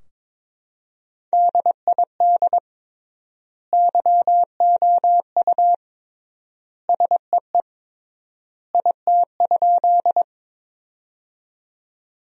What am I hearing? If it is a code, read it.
Morse code, heard as a keyed tone, about 22 wpm, 700 Hz. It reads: DID YOU SEE IT?
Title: 3x Word-Spacing - Sentences from Top 100 Words - 22wpm